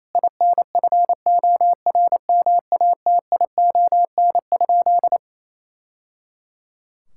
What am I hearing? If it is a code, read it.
Morse code, heard as a keyed tone, about 28 wpm, 700 Hz. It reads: INFORMATION?
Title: Sentences from Top 1000 Words - Review Part 6 - 28wpm